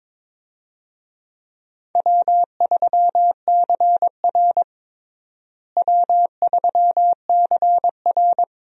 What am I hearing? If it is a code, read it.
Morse code, heard as a keyed tone, about 22 wpm, 700 Hz. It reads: W3CR W3CR
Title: USA Call Signs - 4 Characters - Repeated 2x - 22wpm